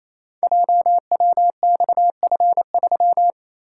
Morse code, heard as a keyed tone, about 29 wpm, 700 Hz. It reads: JWXF3
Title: Code Groups - 5 Letter:Numbers - 28wpm